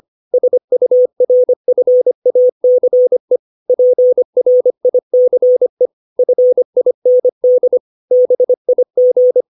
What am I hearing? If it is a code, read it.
Morse code, heard as a keyed tone, about 25 wpm, 500 Hz. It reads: SURFACE PRICE FIND BIG
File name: 632 words / 632 Words 25WPM